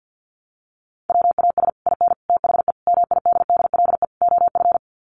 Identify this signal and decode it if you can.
Morse code, heard as a keyed tone, about 50 wpm, 700 Hz. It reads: WAS IN THE MIDDLE OF
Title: Sets of 5 Words Encore 50wpm